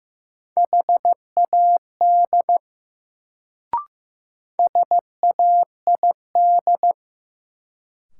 Morse code, heard as a keyed tone, about 16 wpm, 700 Hz. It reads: HAD SAID
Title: Sentences from Top 100 Words - New Word Drill - 15wpm